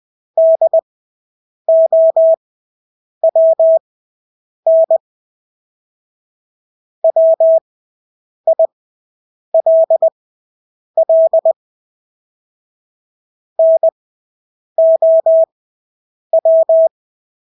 Morse code, heard as a keyed tone, about 20 wpm, spaced as for 8 wpm, 650 Hz. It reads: DOWN WILL NOW
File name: Engelske ord 20/8 wpm